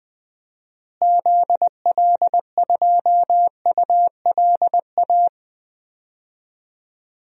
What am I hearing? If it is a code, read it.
Morse code, heard as a keyed tone, about 20 wpm, 700 Hz. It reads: ZL2ULA
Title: DXCC-100 Call Signs - 6 Characters - Repeated 2x - 20wpm